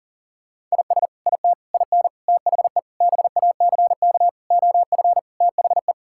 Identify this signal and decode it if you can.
Morse code, heard as a keyed tone, about 40 wpm, 700 Hz. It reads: IS IT IN THE BACK OF THE
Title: Sentences from Top 1000 Words - Review Part 5 - 40wpm